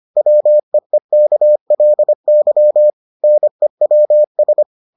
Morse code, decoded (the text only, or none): WEEKLY NEWS